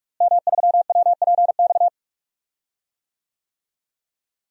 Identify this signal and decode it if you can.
Morse code, heard as a keyed tone, about 41 wpm, 700 Hz. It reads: M3WPX
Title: DXCC-100 Call Signs - 5 Characters - Repeated 3x - 45wpm